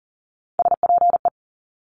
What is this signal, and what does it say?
Morse code, heard as a keyed tone, about 42 wpm, 700 Hz. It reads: SPE
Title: Top Three Letter Combinations in Words 40wpm